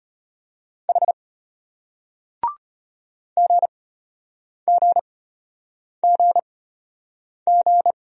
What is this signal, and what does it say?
Morse code, heard as a keyed tone, about 33 wpm, spaced as for 8 wpm, 700 Hz. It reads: H GGGG